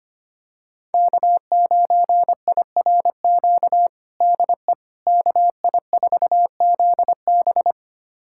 Morse code, heard as a keyed tone, about 25 wpm, 700 Hz. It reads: K9IRQ DE KI4ZB
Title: USA Callsign DE Callsign - 5 Characters - 25wpm